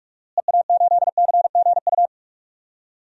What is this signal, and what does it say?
Morse code, heard as a keyed tone, about 42 wpm, 700 Hz. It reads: EA8CGU
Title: DXCC-100 Call Signs - 6 Characters - Repeated 2x - 45wpm